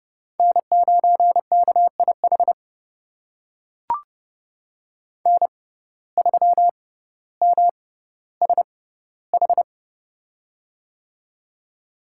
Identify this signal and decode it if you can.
Morse code, heard as a keyed tone, about 30 wpm, spaced as for 15 wpm, 700 Hz. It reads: N9KIH N3MSH
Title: USA Call Signs - 5 Characters - 6x Character Spacing - 30wpm